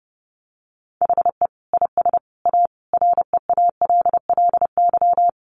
Morse code, heard as a keyed tone, about 30 wpm, 700 Hz. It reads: HE IS A REALLY
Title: Sentences from Top 1200 Words 30wpm